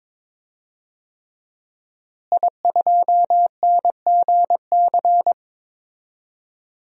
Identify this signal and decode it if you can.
Morse code, heard as a keyed tone, about 22 wpm, 700 Hz. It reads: I2NGC
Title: DXCC-100 Call Signs - 5 Characters - Repeated 3x - 22wpm